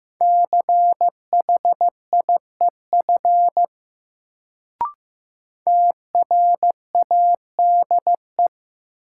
Morse code, heard as a keyed tone, about 15 wpm, 700 Hz. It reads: CHIEF TRADE
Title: Sets of 5 Words Encore - New Word Drill - 15wpm